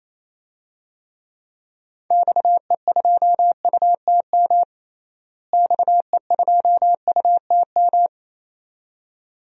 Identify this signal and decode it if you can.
Morse code, heard as a keyed tone, about 28 wpm, 700 Hz. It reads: XE2UTM XE2UTM
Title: DXCC-100 Call Signs - 6 Characters - Repeated 2x - 28wpm